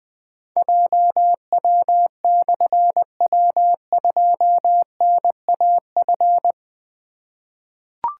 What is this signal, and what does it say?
Morse code, heard as a keyed tone, about 20 wpm, 700 Hz. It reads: JW/W2NAF
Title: Difficult Call Signs 20wpm